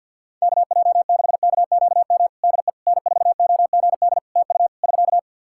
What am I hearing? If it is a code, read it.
Morse code, heard as a keyed tone, about 51 wpm, 700 Hz. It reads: KJ6KQM DE N4OGD TU <SK>
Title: Practice QSO - Exchange 4 - Ending - 50wpm